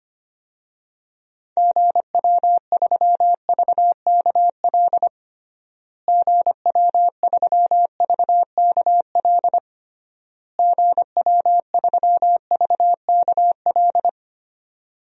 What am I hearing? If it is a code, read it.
Morse code, heard as a keyed tone, about 25 wpm, 700 Hz. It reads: GW3VKL GW3VKL GW3VKL
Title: DXCC-100 Call Signs - 6 Characters - Repeated 3x - 25wpm